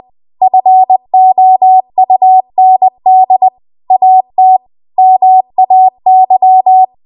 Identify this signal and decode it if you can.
Morse code, heard as a keyed tone, about 20 wpm, 750 Hz. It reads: FOUND AT MAY
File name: Archive / 20 WPM / 20WPM